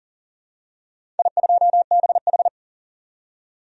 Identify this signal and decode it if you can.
Morse code, heard as a keyed tone, about 40 wpm, 700 Hz. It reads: I2BH
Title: DXCC-100 Call Signs - 4 Characters - 40wpm